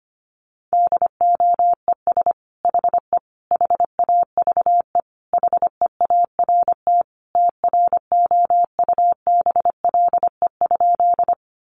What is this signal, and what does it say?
Morse code, heard as a keyed tone, about 25 wpm, 700 Hz. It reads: DOES HE HAVE HEART TROUBLE?
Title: Sentences from Top 1200 Words 25wpm